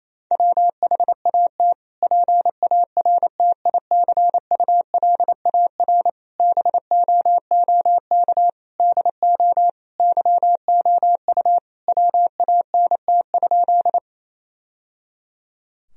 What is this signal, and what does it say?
Morse code, heard as a keyed tone, about 28 wpm, 700 Hz. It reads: WHAT PARTICULAR BOOK DO YOU WANT?
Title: Sentences from Top 1000 Words - Review Part 4 - 28wpm